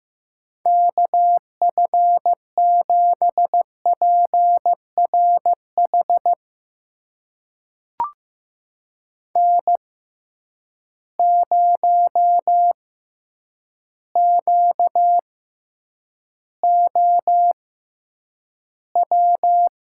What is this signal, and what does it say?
Morse code, heard as a keyed tone, about 15 wpm, spaced as for 8 wpm, 700 Hz. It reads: KF7PRH N0QOW
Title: USA Call Signs - 6x Character Spacing - 15wpm